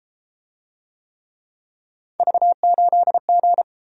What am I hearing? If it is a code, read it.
Morse code, heard as a keyed tone, about 31 wpm, 700 Hz. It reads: V8G